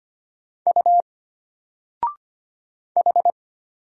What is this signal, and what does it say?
Morse code, heard as a keyed tone, about 25 wpm, 700 Hz. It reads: U H
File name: Single-Letter-Number-25wpm